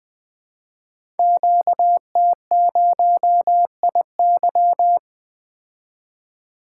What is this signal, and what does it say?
Morse code, heard as a keyed tone, about 20 wpm, 700 Hz. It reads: QT0IY
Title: Code Groups - 5 Letter:Numbers - 20wpm